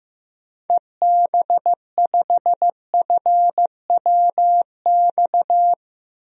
Morse code, heard as a keyed tone, about 15 wpm, 700 Hz. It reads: EB5FWX
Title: DXCC-100 Call Signs - 6 Characters - 15wpm